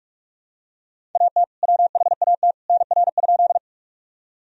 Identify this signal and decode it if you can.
Morse code, heard as a keyed tone, about 46 wpm, 700 Hz. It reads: AT WHAT NR?
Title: Sentences from Top 100 Words and Abbreviations 45wpm